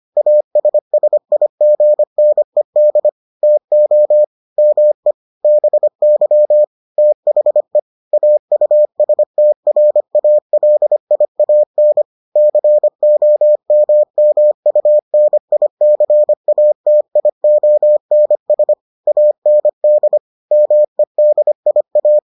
Morse code, heard as a keyed tone, about 25 wpm, 600 Hz. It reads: ASSIGNED TO ME BY THE AUSTRALIAN COMMUNICATIONS AND MEDIA